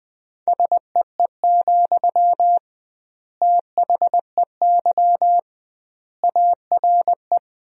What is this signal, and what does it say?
Morse code, heard as a keyed tone, about 21 wpm, 700 Hz. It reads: SEE, THEY ARE
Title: 2x Word-Spacing - Sentences from Top 200 Words - 20wpm